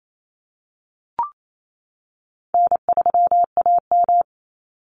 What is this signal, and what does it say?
Morse code, heard as a keyed tone, about 28 wpm, 700 Hz. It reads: N3AM
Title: Call Signs - 28wpm